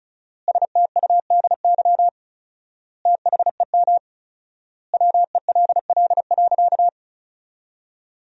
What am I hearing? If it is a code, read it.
Morse code, heard as a keyed tone, about 34 wpm, 700 Hz. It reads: STUDY THEM WELL.
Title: 4x Word-Spacing - Sentences from Top 300 Words - 35wpm